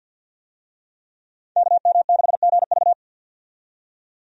Morse code, heard as a keyed tone, about 51 wpm, 700 Hz. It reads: KM6GV
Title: USA Call Signs - 5 Characters - 50wpm